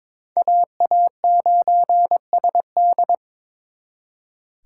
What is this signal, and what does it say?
Morse code, heard as a keyed tone, about 26 wpm, 700 Hz. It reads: AA9SD
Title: Speed Racing - USA 2x2 Call Signs 22wpm